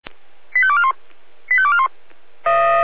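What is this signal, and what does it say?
Five-tone selective call (ZVEI), heard as quick runs of five tones, tones of about 70 ms. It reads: 85241 85241
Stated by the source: ZVEI five-tone selective call, address 85340, received by SDR.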